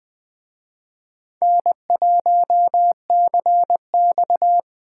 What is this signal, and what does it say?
Morse code, heard as a keyed tone, about 21 wpm, 700 Hz. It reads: N1CX
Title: Speed Racing - USA Call Signs 17wpm